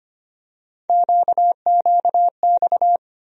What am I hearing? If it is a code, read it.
Morse code, heard as a keyed tone, about 25 wpm, 700 Hz. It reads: QQX
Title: USA Call Sign Postfixes - 3 Letters 25wpm